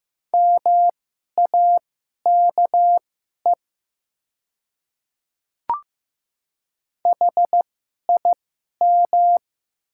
Morse code, heard as a keyed tone, about 15 wpm, spaced as for 11 wpm, 700 Hz. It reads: MAKE HIM
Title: 2x Character Spacing - Top 100 Words - 15wpm